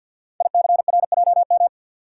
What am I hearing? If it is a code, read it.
Morse code, heard as a keyed tone, about 50 wpm, 700 Hz. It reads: ICRJM
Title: Code Groups - 5 Letters - 50wpm